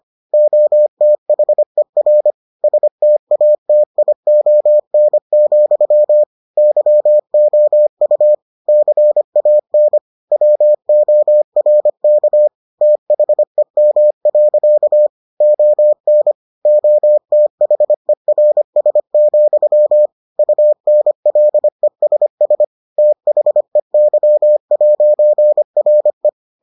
Morse code, heard as a keyed tone, about 25 wpm, 600 Hz. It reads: OTHER STATION, YOU CAN WORK THEM. ON OTHERS, UNLESS THEY'RE